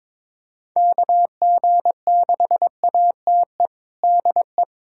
Morse code, heard as a keyed tone, about 23 wpm, 700 Hz. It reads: KG6ATE DE
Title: USA Callsign DE Callsign 22wpm